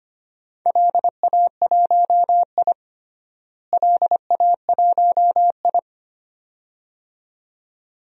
Morse code, heard as a keyed tone, about 25 wpm, 700 Hz. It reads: LA1I LA1I
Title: DXCC-100 Call Signs - 4 Characters - Repeated 2x - 25wpm